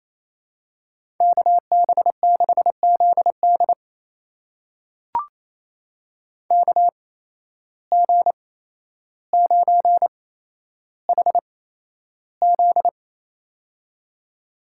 Morse code, heard as a keyed tone, about 28 wpm, spaced as for 12 wpm, 700 Hz. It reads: KB6ZD KG9HZ
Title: USA Call Signs - 5 Characters - 8x Character Spacing - 28wpm